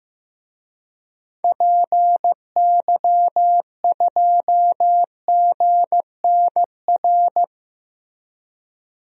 Morse code, heard as a keyed tone, about 15 wpm, 700 Hz. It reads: PY2GNR
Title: DXCC-100 Call Signs - 6 Characters - Repeated 2x - 15wpm